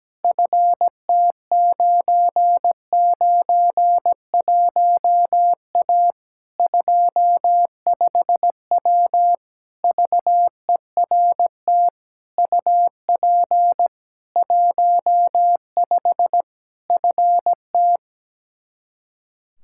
Morse code, heard as a keyed tone, about 17 wpm, 700 Hz. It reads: FT991A 25W VERT UP 15 FT
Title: Practice QSO - Exchange 2 - Info - 17wpm